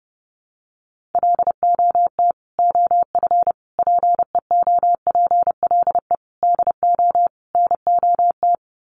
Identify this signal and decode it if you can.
Morse code, heard as a keyed tone, about 30 wpm, 700 Hz. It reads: LOT OF PEOPLE DO NOT
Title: Sets of 5 Words Encore 30wpm